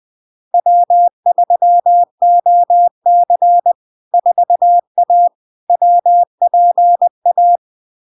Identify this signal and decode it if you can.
Morse code, heard as a keyed tone, about 20 wpm, 700 Hz. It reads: W3OC 4A WPA
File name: LARK-FD-2022-Log-in-CW-20wpm-700-Hz